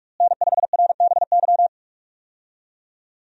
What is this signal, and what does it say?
Morse code, heard as a keyed tone, about 42 wpm, 700 Hz. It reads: N5RBY
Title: USA Call Signs - 5 Characters - Repeated 3x - 45wpm